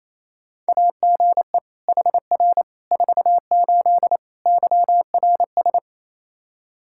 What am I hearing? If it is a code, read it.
Morse code, heard as a keyed tone, about 27 wpm, 700 Hz. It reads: AGE HR 48 YRS
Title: Practice QSO - Exchange 3 - More Info - 28wpm